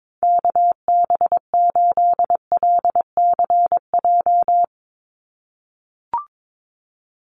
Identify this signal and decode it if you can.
Morse code, heard as a keyed tone, about 22 wpm, 700 Hz. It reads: KB8LCJ
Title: USA Call Signs 22wpm